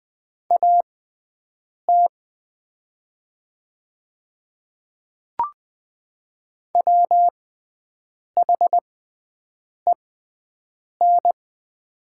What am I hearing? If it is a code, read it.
Morse code, heard as a keyed tone, about 20 wpm, spaced as for 7 wpm, 700 Hz. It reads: AT WHEN
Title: 6x Character Spacing - Top 100 Words - 20wpm